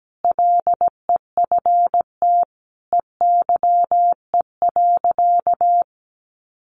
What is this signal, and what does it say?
Morse code, heard as a keyed tone, about 17 wpm, 700 Hz. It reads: LEFT EYE.
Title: Sentences from Top 1400 Words 17wpm